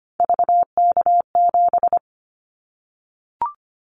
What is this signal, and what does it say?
Morse code, heard as a keyed tone, about 25 wpm, 700 Hz. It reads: VK7
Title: Callsign Prefixes for Top 100 DXCC - 25wpm